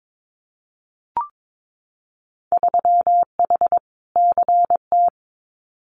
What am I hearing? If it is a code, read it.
Morse code, heard as a keyed tone, about 22 wpm, 700 Hz. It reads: 3H CT
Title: Winter Field Day Exchanges Encore 22wpm